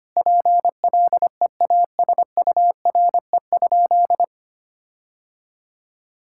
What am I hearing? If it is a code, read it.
Morse code, heard as a keyed tone, about 25 wpm, 700 Hz. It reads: PLEASURE?
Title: Sentences from Top 1600 Words 25wpm